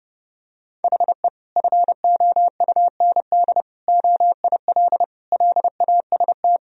Morse code, heard as a keyed tone, about 32 wpm, 700 Hz. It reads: HE FOUND OIL LAST
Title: Sentences from Top 1000 Words - Review Part 4 - 30wpm